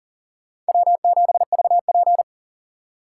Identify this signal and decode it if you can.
Morse code, heard as a keyed tone, about 40 wpm, 700 Hz. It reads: W7VP
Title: K1USN SST QSO Practice - 40wpm